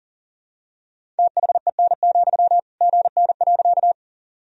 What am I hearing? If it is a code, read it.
Morse code, heard as a keyed tone, about 40 wpm, 700 Hz. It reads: THEN, GN.
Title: Sentences from Top 300 Words and Abbreviations 40wpm